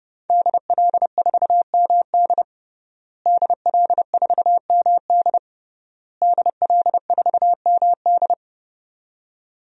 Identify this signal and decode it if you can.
Morse code, heard as a keyed tone, about 30 wpm, 700 Hz. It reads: DL4MD DL4MD DL4MD
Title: DXCC-100 Call Signs - 5 Characters - Repeated 3x - 30wpm